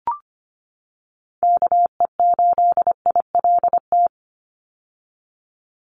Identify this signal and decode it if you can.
Morse code, heard as a keyed tone, about 25 wpm, 700 Hz. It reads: KE8ILT
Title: USA Call Signs 25wpm